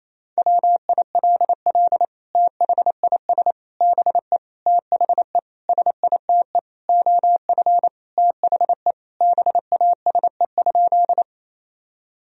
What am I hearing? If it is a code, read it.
Morse code, heard as a keyed tone, about 27 wpm, 700 Hz. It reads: WILL THIS BE THE SITE OF THE BASE?
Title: Sentences from Top 1000 Words - Review Part 1 - 28wpm